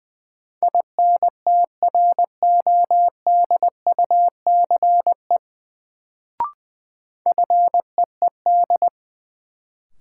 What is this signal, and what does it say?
Morse code, heard as a keyed tone, about 20 wpm, 700 Hz. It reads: INTRODUCE FEED
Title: Sentences from Top 1600 Words - New Word Drill - 20wpm